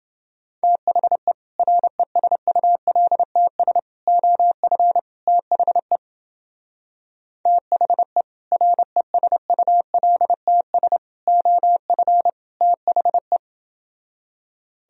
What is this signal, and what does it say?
Morse code, heard as a keyed tone, about 29 wpm, 700 Hz. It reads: THE RESULTS OF THE THE RESULTS OF THE